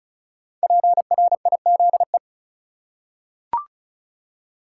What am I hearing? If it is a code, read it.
Morse code, heard as a keyed tone, about 34 wpm, 700 Hz. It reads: PRIZE
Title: Five Letter Words - 35wpm